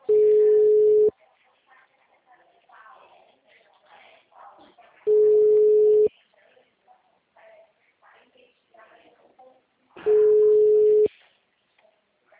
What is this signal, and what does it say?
Ringback tone (Europe, 425 Hz), with speech.